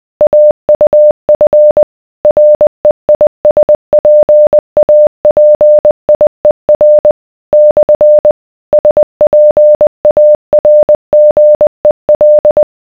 Morse code, heard as a keyed tone, about 20 wpm, 600 Hz. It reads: AUF REISPAPIER / SPARGEL